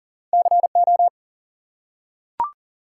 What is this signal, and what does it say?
Morse code, heard as a keyed tone, about 39 wpm, 700 Hz. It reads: CO
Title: US States and Canadian Provinces 40wpm